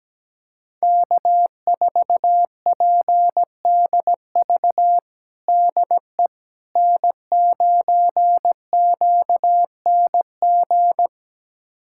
Morse code, heard as a keyed tone, about 17 wpm, 700 Hz. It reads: K4PDV DE N9QNG
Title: USA Callsign DE Callsign 17wpm